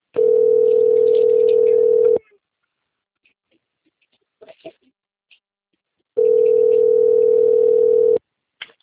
Ringback tone (North America), 440 + 480 Hz.